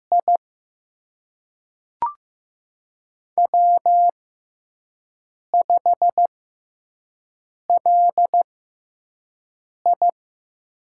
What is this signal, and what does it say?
Morse code, heard as a keyed tone, about 15 wpm, spaced as for 5 wpm, 700 Hz. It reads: I W5LI
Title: USA Call Signs - 4 Characters - 6x Character Spacing - 15wpm